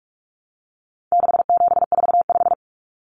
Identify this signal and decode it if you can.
Morse code, heard as a keyed tone, about 44 wpm, 700 Hz. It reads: 6745
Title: Common Cut Numbers - 4-Digit 45wpm